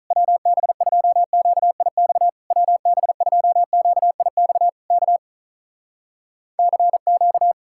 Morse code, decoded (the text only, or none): WB2QIX WB2QIX K CQ